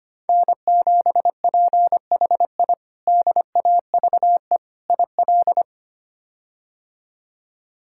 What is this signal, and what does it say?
Morse code, heard as a keyed tone, about 25 wpm, 700 Hz. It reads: N7PHI DAVE IL
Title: K1USN SST QSO Practice - 25wpm